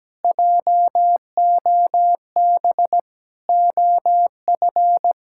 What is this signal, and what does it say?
Morse code, heard as a keyed tone, about 17 wpm, 700 Hz. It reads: JOB OF
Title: Pangram Challenge v2 - 17wpm